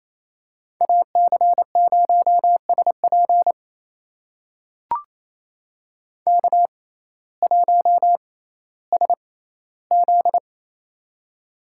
Morse code, heard as a keyed tone, about 28 wpm, spaced as for 15 wpm, 700 Hz. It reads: AC0SP K1SZ
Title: USA Call Signs - 6x Character Spacing - 28wpm